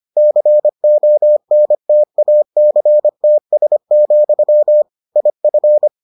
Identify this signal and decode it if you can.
Morse code, heard as a keyed tone, about 25 wpm, 600 Hz. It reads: CONTACTS, IF